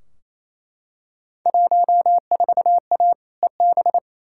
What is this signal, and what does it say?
Morse code, heard as a keyed tone, about 28 wpm, 700 Hz. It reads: 14A EB
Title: Field Day Exchanges Encore - 28wpm